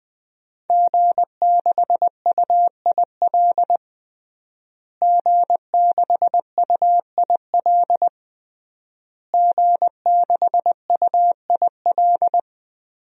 Morse code, heard as a keyed tone, about 20 wpm, 700 Hz. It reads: G6UIL G6UIL G6UIL
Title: DXCC-100 Call Signs - 5 Characters - Repeated 3x - 20wpm